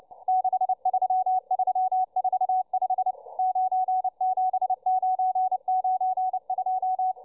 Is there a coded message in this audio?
Yes: Morse code from 0.3 to 7.1 s, heard as a keyed tone, about 29 wpm, 750 Hz, with radio noise. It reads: 63345 97992